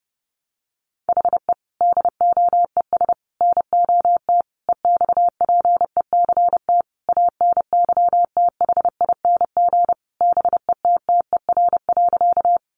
Morse code, heard as a keyed tone, about 30 wpm, 700 Hz. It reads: HE DOES NOT EXPECT ANYTHING BETTER.
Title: Sentences from Top 1100 Words 30wpm